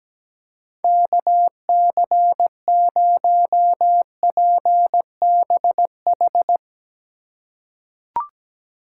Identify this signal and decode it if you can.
Morse code, heard as a keyed tone, about 17 wpm, 700 Hz. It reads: KC0PBH
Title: USA Call Signs - 6 Characters - 2x Character Spacing - 17wpm